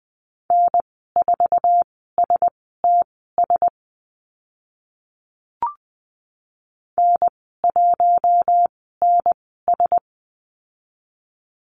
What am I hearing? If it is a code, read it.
Morse code, heard as a keyed tone, about 20 wpm, spaced as for 14 wpm, 700 Hz. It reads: N4STS N1NS
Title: 048 Lesson 9g - Number 4 - Call Signs 20wpm